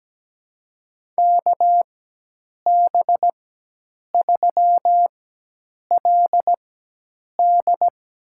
Morse code, heard as a keyed tone, about 17 wpm, spaced as for 8 wpm, 700 Hz. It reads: KB3LD
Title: USA Call Signs - 4x Character Spacing - 17wpm